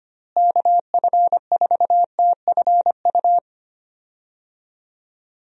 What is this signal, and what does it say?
Morse code, heard as a keyed tone, about 25 wpm, 700 Hz. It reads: KF4TFU
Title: USA Call Signs - 6 Characters - Repeated 3x - 25wpm